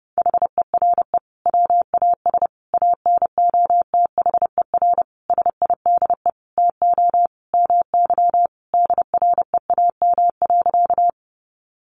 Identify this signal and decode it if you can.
Morse code, heard as a keyed tone, about 30 wpm, 700 Hz. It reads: HERE WAS ANOTHER SIDE TO MY DREAM.
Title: Sentences from Top 1500 Words 30wpm